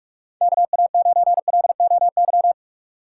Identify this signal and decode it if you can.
Morse code, heard as a keyed tone, about 46 wpm, 700 Hz. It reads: KA9LOY
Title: USA Call Signs - 6 Characters - 8x Character Spacing - 45wpm